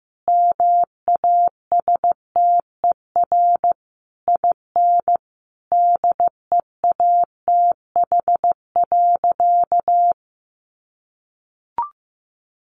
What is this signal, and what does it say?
Morse code, heard as a keyed tone, about 15 wpm, 700 Hz. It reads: MASTER IN DEATH.
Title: Sentences from Top 1300 Words 15wpm